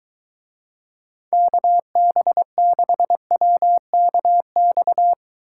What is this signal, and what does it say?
Morse code, heard as a keyed tone, about 21 wpm, 700 Hz. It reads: KB6WKX